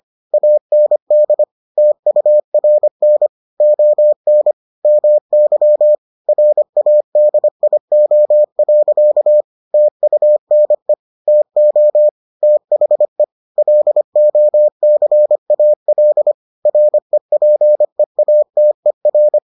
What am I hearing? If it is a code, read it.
Morse code, heard as a keyed tone, about 25 wpm, 600 Hz. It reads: AND TURN ON MY RADIO. TUNE TO THE LOCAL REPEATER